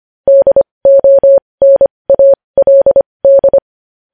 Morse code, heard as a keyed tone, about 25 wpm, spaced as for 20 wpm, 550 Hz. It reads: DONALD
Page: news headlines in morse code at 20 wpm